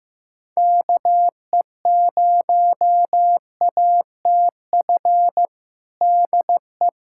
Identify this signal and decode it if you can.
Morse code, heard as a keyed tone, about 15 wpm, 700 Hz. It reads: KE0ATF DE
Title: USA Callsign DE Callsign 15wpm